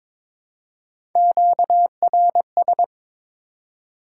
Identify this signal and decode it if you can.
Morse code, heard as a keyed tone, about 22 wpm, 700 Hz. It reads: QRS
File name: Top 90 QSO Elements - 3x Character Spacing - 22wpm